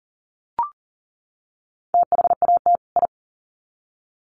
Morse code, heard as a keyed tone, about 40 wpm, 700 Hz. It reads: THAT I
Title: Sets of 2 Words 40wpm